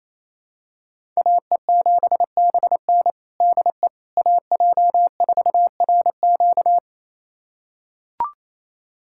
Morse code, heard as a keyed tone, about 28 wpm, 700 Hz. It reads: AE7BN DE AJ4RQ
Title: USA Callsign DE Callsign - 5 Characters - 28wpm